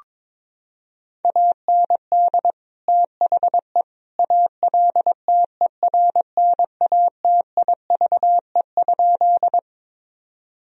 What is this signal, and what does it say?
Morse code, heard as a keyed tone, about 22 wpm, 700 Hz. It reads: AND THE ALTERNATIVE?
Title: Sentences from Top 1000 Words - Review Part 4 - 22wpm